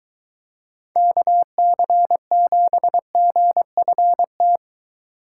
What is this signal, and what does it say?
Morse code, heard as a keyed tone, about 22 wpm, 700 Hz. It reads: KC7GFT